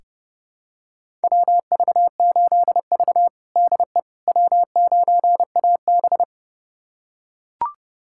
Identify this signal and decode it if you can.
Morse code, heard as a keyed tone, about 30 wpm, 700 Hz. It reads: WV8V DE W9AB
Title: USA Callsign DE Callsign - 4 Characters - 30wpm